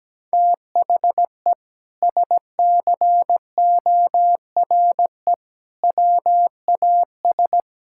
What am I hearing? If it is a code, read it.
Morse code, heard as a keyed tone, about 17 wpm, 700 Hz. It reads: THE SCORE WAS
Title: Sentences from Top 1600 Words 17wpm